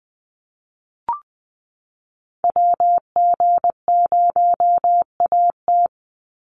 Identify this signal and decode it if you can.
Morse code, heard as a keyed tone, about 20 wpm, 700 Hz. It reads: WG0AT
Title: NA SOTA Activator Callsigns 20wpm